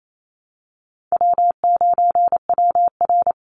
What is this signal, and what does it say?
Morse code, heard as a keyed tone, about 28 wpm, 700 Hz. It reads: W9WR
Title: Easy Call Signs - 28wpm